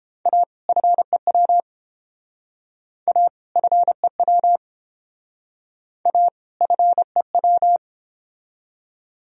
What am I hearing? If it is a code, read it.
Morse code, heard as a keyed tone, about 30 wpm, 700 Hz. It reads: A FEW A FEW A FEW